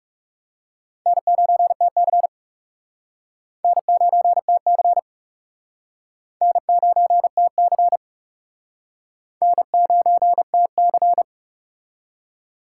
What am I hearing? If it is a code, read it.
Morse code, heard as a keyed tone, about 36 wpm, 700 Hz. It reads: N9TC N9TC N9TC N9TC
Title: Speed Racing - USA 1x2 Call Signs 30wpm